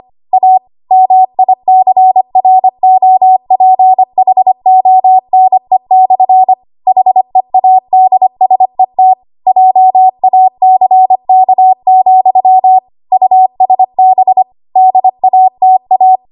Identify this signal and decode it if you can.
Morse code, heard as a keyed tone, about 25 wpm, 750 Hz. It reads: A MICROPHONE/ HEADSET JACK, USB DATA
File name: Archive / 25 WPM / 25WPM